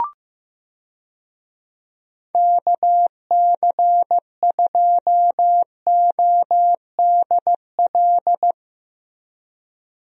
Morse code, heard as a keyed tone, about 15 wpm, 700 Hz. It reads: KC2ODL
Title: USA Call Signs - 6 Characters - Repeated 2x - 15wpm